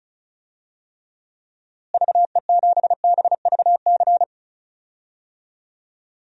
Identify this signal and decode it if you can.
Morse code, heard as a keyed tone, about 36 wpm, 700 Hz. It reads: VE7BVC
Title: DXCC-100 Call Signs - 6 Characters - Repeated 2x - 35wpm